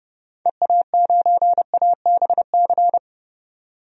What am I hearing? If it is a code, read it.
Morse code, heard as a keyed tone, about 31 wpm, 700 Hz. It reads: EA9ABC
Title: DXCC-100 Call Signs - 6 Characters - 30wpm